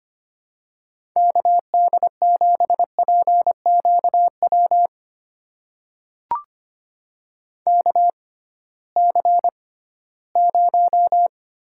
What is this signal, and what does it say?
Morse code, heard as a keyed tone, about 25 wpm, spaced as for 16 wpm, 700 Hz. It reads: KD7PQW KC0
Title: USA Call Signs - 6 Characters - 6x Character Spacing - 25wpm